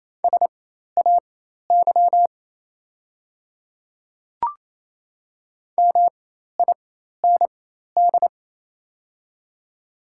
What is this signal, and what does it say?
Morse code, heard as a keyed tone, about 28 wpm, spaced as for 13 wpm, 700 Hz. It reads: SAY MIND
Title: 4x Character Spacing - Top 500 Words - 28wpm